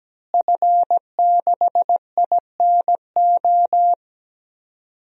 Morse code, heard as a keyed tone, about 17 wpm, 700 Hz. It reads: F6INO
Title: DXCC-100 Call Signs - 5 Characters - Repeated 2x - 17wpm